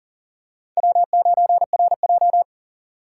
Morse code, heard as a keyed tone, about 40 wpm, 700 Hz. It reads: W9RJ
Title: USA Call Signs - 4 Characters - 4x Character Spacing - 40wpm